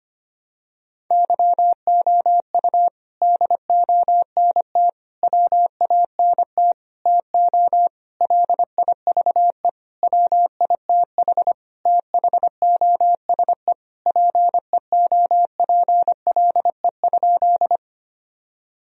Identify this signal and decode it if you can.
Morse code, heard as a keyed tone, about 25 wpm, 700 Hz. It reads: YOU DONT WANT TO LIVE WITH THOSE PEOPLE?
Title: Sentences from Top 1000 Words - Review Part 1 - 25wpm